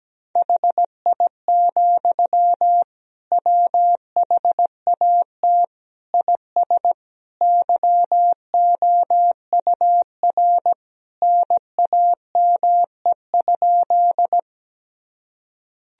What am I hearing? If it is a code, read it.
Morse code, heard as a keyed tone, about 17 wpm, 700 Hz. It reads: HI, WHAT IS YOUR NAME?
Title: Sentences from Top 1000 Words - Review Part 3 - 17wpm